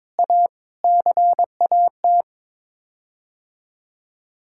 Morse code, heard as a keyed tone, about 22 wpm, 700 Hz. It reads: A CAT
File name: Sets of 9 Words - 22wpm